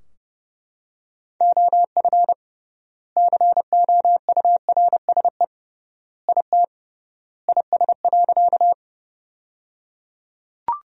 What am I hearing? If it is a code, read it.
Morse code, heard as a keyed tone, about 29 wpm, 700 Hz. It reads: OF COURSE IT IS.
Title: 3x Word-Spacing - Sentences from Top 500 Words - 30wpm